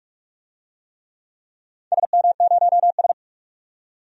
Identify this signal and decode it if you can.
Morse code, heard as a keyed tone, about 46 wpm, 700 Hz. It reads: SM0S
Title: DXCC-100 Call Signs - 4 Characters - 45wpm